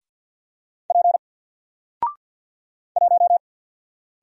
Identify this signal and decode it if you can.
Morse code, heard as a keyed tone, about 48 wpm, 700 Hz. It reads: P 1
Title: Single Letter-Number 50wpm